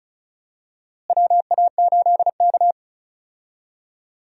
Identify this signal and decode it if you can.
Morse code, heard as a keyed tone, about 36 wpm, 700 Hz. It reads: WA8K
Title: USA Call Signs - 4 Characters - Repeated 2x - 35wpm